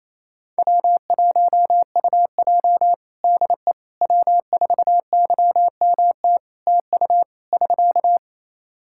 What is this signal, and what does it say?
Morse code, heard as a keyed tone, about 28 wpm, 700 Hz. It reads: W1UJ DE W4YMT TU <SK>
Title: Practice QSO - Exchange 4 - Ending - 28wpm